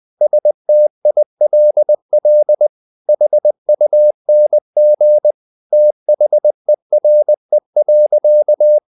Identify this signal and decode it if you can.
Morse code, heard as a keyed tone, about 20 wpm, 600 Hz. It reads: STILL HUNG THERE.
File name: the old man and the sea part 2 20wpm